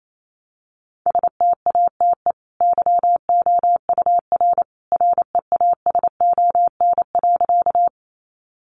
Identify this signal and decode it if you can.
Morse code, heard as a keyed tone, about 28 wpm, 700 Hz. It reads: STATE YOUR REASON.
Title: Sentences from Top 700 Words 28wpm